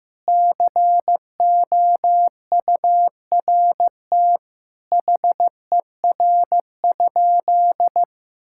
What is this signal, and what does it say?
Morse code, heard as a keyed tone, about 15 wpm, 700 Hz. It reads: COURT HER?
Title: Sentences from Top 1600 Words 15wpm